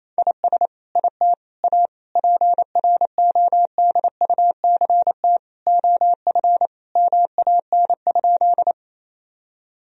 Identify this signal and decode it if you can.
Morse code, heard as a keyed tone, about 28 wpm, 700 Hz. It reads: IS IT A PRODUCT OF MAN?
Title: Sentences from Top 1000 Words - Review Part 2 - 28wpm